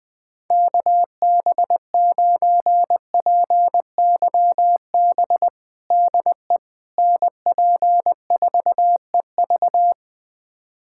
Morse code, heard as a keyed tone, about 20 wpm, 700 Hz. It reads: KB9PYB DE NP4EV
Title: USA Callsign DE Callsign 20wpm